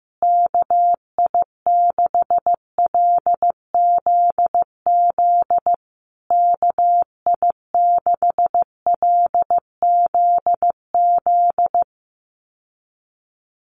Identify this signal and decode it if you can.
Morse code, heard as a keyed tone, about 15 wpm, 700 Hz. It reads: KI6LZZ KI6LZZ
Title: USA Call Signs Repeated 3x 15wpm